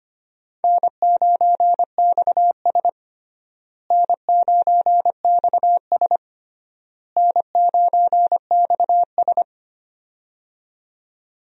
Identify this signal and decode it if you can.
Morse code, heard as a keyed tone, about 25 wpm, 700 Hz. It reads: N9XS N9XS N9XS
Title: USA Call Signs - 4 Characters - Repeated 3x - 25wpm